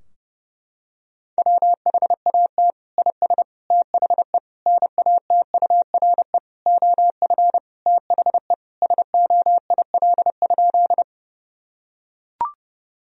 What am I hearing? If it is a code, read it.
Morse code, heard as a keyed tone, about 30 wpm, 700 Hz. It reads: WHAT IS THE NATURE OF THE SOIL?
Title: Sentences from Top 1000 Words - Review Part 4 - 30wpm